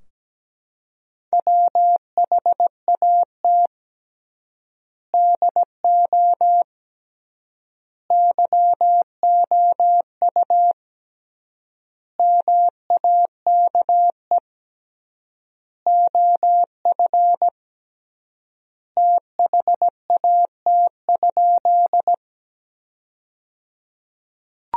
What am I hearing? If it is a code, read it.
Morse code, heard as a keyed tone, about 17 wpm, 700 Hz. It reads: WHAT DO YOU MAKE OF THAT?
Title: 3x Word-Spacing - Sentences from Top 100 Words - 17wpm